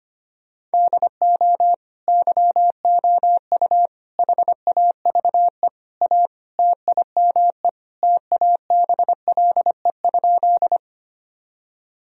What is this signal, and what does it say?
Morse code, heard as a keyed tone, about 25 wpm, 700 Hz. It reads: DO YOU HAVE A TIME TABLE?
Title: Sentences from Top 1000 Words - Review Part 2 - 25wpm